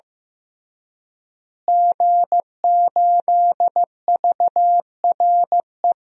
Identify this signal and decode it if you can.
Morse code, heard as a keyed tone, about 16 wpm, 700 Hz. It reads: G8VRE
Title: DXCC-100 Call Signs - 5 Characters - Repeated 3x - 15wpm